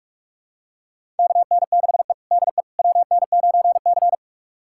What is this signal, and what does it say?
Morse code, heard as a keyed tone, about 46 wpm, 700 Hz. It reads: KN6E DE WN9C
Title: USA Callsign DE Callsign - 4 Characters - 45wpm